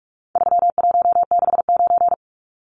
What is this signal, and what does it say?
Morse code, heard as a keyed tone, about 44 wpm, 700 Hz. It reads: 3169
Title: Numbers - 4-Digit 45wpm